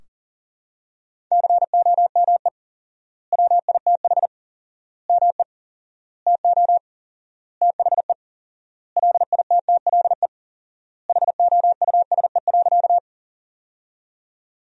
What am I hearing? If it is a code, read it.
Morse code, heard as a keyed tone, about 40 wpm, 700 Hz. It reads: COME WITH ME TO THE LITTLE HOUSE.
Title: 4x Word-Spacing - Sentences from Top 200 Words - 40wpm